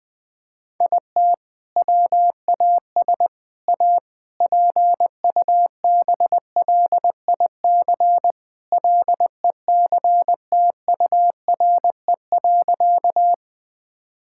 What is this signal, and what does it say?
Morse code, heard as a keyed tone, about 20 wpm, 700 Hz. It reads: IT WAS A PUBLIC LECTURE.
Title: Sentences from Top 1600 Words 20wpm